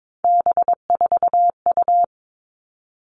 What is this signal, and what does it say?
Morse code, heard as a keyed tone, about 22 wpm, 700 Hz. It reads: B4U
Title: All Cut Numbers - 3-Digit 22wpm